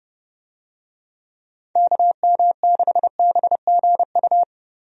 Morse code, heard as a keyed tone, about 30 wpm, 700 Hz. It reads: KM6BGU
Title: USA Call Signs - 6 Characters - 30wpm